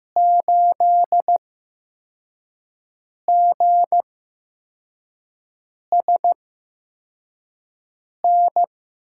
Morse code, heard as a keyed tone, about 15 wpm, spaced as for 4 wpm, 700 Hz. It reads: 8GSN
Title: USA Call Signs - 5 Characters - 8x Character Spacing - 15wpm